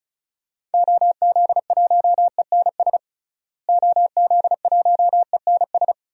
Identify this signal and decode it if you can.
Morse code, heard as a keyed tone, about 35 wpm, 700 Hz. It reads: OZ1ENS OZ1ENS
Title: DXCC-100 Call Signs - 6 Characters - Repeated 2x - 35wpm